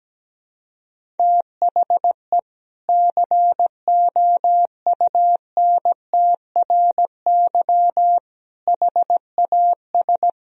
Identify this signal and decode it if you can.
Morse code, heard as a keyed tone, about 17 wpm, 700 Hz. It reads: THE COUNTRY HAS
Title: Sentences from Top 1000 Words - Review Part 1 - 17wpm